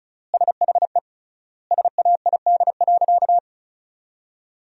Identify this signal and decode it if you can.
Morse code, heard as a keyed tone, about 36 wpm, 700 Hz. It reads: SHE SAID.
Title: 3x Word-Spacing - Sentences from Top 400 Words - 35wpm